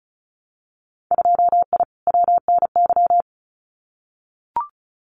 Morse code, heard as a keyed tone, about 36 wpm, 700 Hz. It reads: 2I WNY
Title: Winter Field Day Exchanges Encore 35wpm